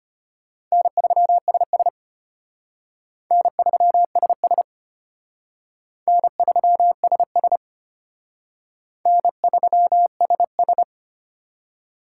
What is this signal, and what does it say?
Morse code, heard as a keyed tone, about 33 wpm, 700 Hz. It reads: N3SS N3SS N3SS N3SS